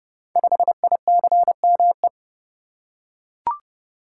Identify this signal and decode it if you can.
Morse code, heard as a keyed tone, about 30 wpm, 700 Hz. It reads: 5ICME
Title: Code Groups - 5 Letter:Numbers - 30wpm